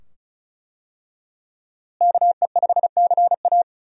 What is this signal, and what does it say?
Morse code, heard as a keyed tone, about 36 wpm, 700 Hz. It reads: KE5CA
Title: USA Call Signs - 5 Characters - Repeated 2x - 35wpm